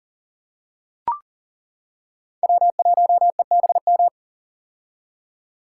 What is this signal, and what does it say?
Morse code, heard as a keyed tone, about 40 wpm, 700 Hz. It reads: W1EBM
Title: K1USN SST QSO Practice - 40wpm